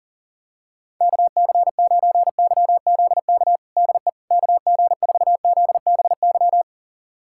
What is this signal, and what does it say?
Morse code, heard as a keyed tone, about 40 wpm, 700 Hz. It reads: KC9YZK DE KG4ZBY
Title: USA Callsign DE Callsign 40wpm